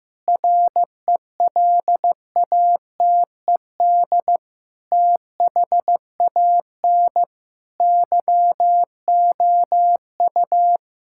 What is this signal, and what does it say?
Morse code, heard as a keyed tone, about 15 wpm, 700 Hz. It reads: RELATED THAN YOU
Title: Sentences from Top 1000 Words - Review Part 5 - 15wpm